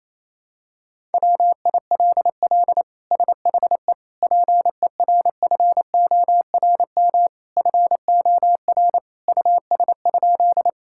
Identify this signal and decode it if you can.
Morse code, heard as a keyed tone, about 28 wpm, 700 Hz. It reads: WILL SHE PERFORM FOR US?
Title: Sentences from Top 1000 Words - Review Part 5 - 28wpm